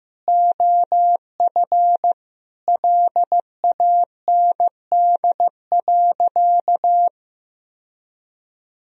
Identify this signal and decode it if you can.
Morse code, heard as a keyed tone, about 15 wpm, 700 Hz. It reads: OF LAND.
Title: Sentences from Top 1000 Words - Review Part 2 - 15wpm